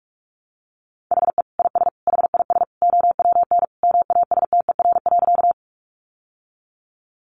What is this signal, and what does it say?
Morse code, heard as a keyed tone, about 44 wpm, 700 Hz. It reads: HE IS HIS OWN MASTER.
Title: Sentences from Top 1300 Words 45wpm